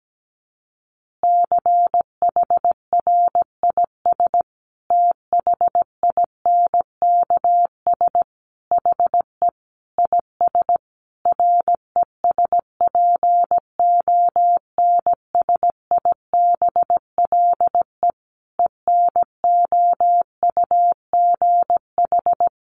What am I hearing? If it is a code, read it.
Morse code, heard as a keyed tone, about 17 wpm, 700 Hz. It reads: CHRIS THINKS HE IS RESPONSIBLE ENOUGH
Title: Easy Phrases 17wpm